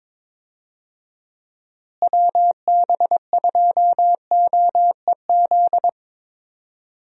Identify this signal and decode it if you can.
Morse code, heard as a keyed tone, about 22 wpm, 700 Hz. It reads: WB2OEZ
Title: USA Call Signs - 6 Characters - Repeated 2x - 22wpm